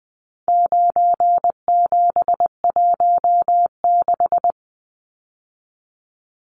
Morse code, heard as a keyed tone, about 20 wpm, 700 Hz. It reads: 9716
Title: Numbers - 4-Digit 20wpm